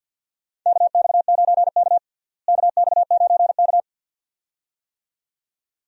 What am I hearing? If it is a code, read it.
Morse code, heard as a keyed tone, about 52 wpm, 700 Hz. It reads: KX9K KX9K
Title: USA Call Signs - 4 Characters - Repeated 2x - 50wpm